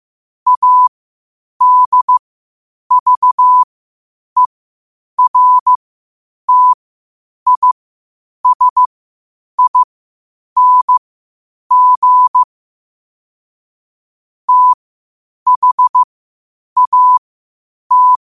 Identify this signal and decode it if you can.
Morse code, heard as a keyed tone, about 15 wpm, spaced as for 8 wpm, 1000 Hz. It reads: ADVERTISING THAT